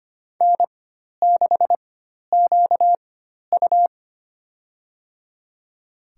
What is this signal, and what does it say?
Morse code, heard as a keyed tone, about 25 wpm, spaced as for 12 wpm, 700 Hz. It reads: N6QU
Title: USA Call Signs - 4 Characters - 4x Character Spacing - 25wpm